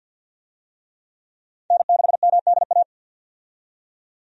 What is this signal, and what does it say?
Morse code, heard as a keyed tone, about 48 wpm, 700 Hz. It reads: N6MDA
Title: USA Call Signs - 5 Characters - 50wpm